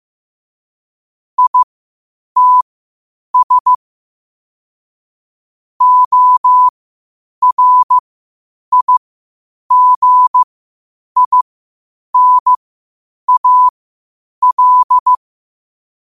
Morse code, heard as a keyed tone, about 15 wpm, spaced as for 8 wpm, 1000 Hz. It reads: ITS ORIGINAL
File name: qotdfiles / 8wpm